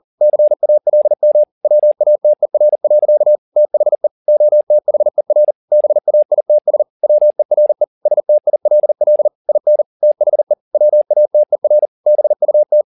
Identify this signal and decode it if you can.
Morse code, heard as a keyed tone, about 40 wpm, 600 Hz. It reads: CALM WATER. THE OTHER BAITS WERE STILL IN THE WATER BUT